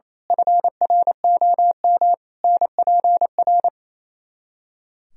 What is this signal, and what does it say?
Morse code, heard as a keyed tone, about 29 wpm, 700 Hz. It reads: FROM NPR
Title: Speed-Racing - Sets of 5 Words 28wpm